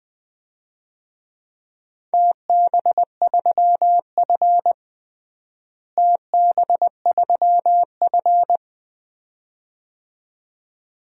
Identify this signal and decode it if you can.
Morse code, heard as a keyed tone, about 20 wpm, 700 Hz. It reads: TB3F TB3F
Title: DXCC-100 Call Signs - 4 Characters - Repeated 2x - 20wpm